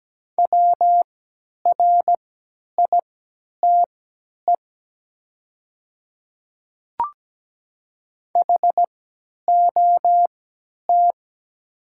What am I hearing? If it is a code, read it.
Morse code, heard as a keyed tone, about 17 wpm, spaced as for 10 wpm, 690 Hz. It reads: WRITE HOT
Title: 3x Character Spacing - Top 100 Words - 17wpm